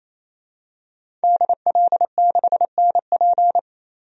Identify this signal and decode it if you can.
Morse code, heard as a keyed tone, about 28 wpm, 700 Hz. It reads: DL6NP
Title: DXCC-100 Call Signs - 5 Characters - 28wpm